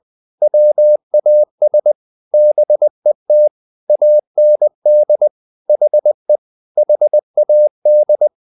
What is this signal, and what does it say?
Morse code, heard as a keyed tone, about 20 wpm, 600 Hz. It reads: WAS BET AND HE HAD
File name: the old man and the sea part 2 20wpm